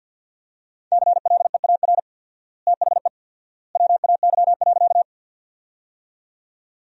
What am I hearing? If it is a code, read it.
Morse code, heard as a keyed tone, about 53 wpm, 700 Hz. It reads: CLEAR THE WAY.
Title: 4x Word-Spacing - Sentences from Top 500 Words - 50wpm